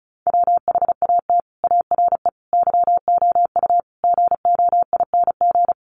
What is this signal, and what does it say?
Morse code, heard as a keyed tone, about 34 wpm, 700 Hz. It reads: WHAT ARE YOU GOING
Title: Sets of 5 Words 35wpm